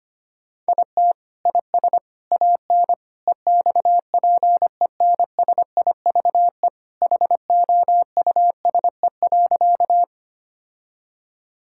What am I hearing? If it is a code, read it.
Morse code, heard as a keyed tone, about 25 wpm, 700 Hz. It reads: IT IS AN EXPENSIVE HOUSE.
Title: Sentences from Top 1000 Words - Review Part 6 - 25wpm